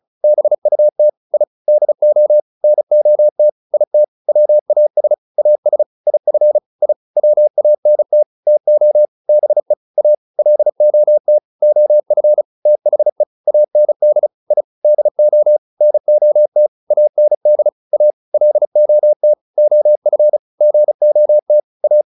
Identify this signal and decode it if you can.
Morse code, heard as a keyed tone, about 34 wpm, 600 Hz. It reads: BUT I DO NOT IT WAS AS IF I WANT TO BE A LOT OF THE AND I DO NOT AND A LOT OF GOT A